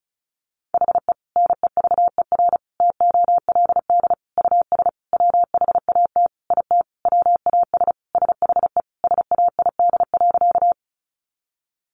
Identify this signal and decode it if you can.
Morse code, heard as a keyed tone, about 34 wpm, 700 Hz. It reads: HE NEVER TOLD US WHAT IT WAS SHE SAID.
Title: Sentences from Top 400 Words 35wpm